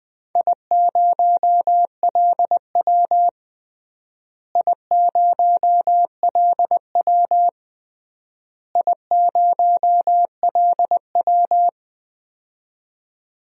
Morse code, heard as a keyed tone, about 20 wpm, 700 Hz. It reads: I0LW I0LW I0LW
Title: DXCC-100 Call Signs - 4 Characters - Repeated 3x - 20wpm